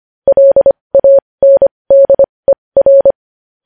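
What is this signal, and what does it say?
Morse code, heard as a keyed tone, about 25 wpm, spaced as for 20 wpm, 550 Hz. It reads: LANDER
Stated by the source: Morse code news headlines sent at 20.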